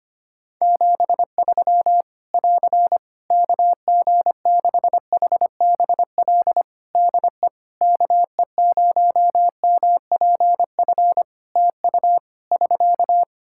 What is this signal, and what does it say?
Morse code, heard as a keyed tone, about 25 wpm, 700 Hz. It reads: 73 <AR> KG6HBL DE KE0MPF TU <SK>
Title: Practice QSO - Exchange 4 - Ending - 25wpm